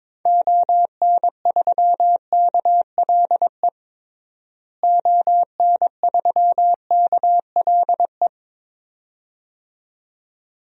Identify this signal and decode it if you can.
Morse code, heard as a keyed tone, about 22 wpm, 700 Hz. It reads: ON3KLE ON3KLE
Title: DXCC-100 Call Signs - 6 Characters - Repeated 2x - 22wpm